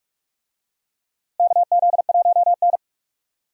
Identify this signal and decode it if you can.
Morse code, heard as a keyed tone, about 47 wpm, 700 Hz. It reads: KZ1N
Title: USA Call Signs - 4 Characters - Repeated 2x - 45wpm